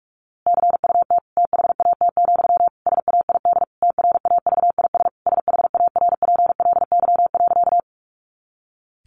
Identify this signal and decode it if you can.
Morse code, heard as a keyed tone, about 44 wpm, 700 Hz. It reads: CUT THAT, SAID TRAVIS SHARPLY.
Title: A Sound Of Thunder 45wpm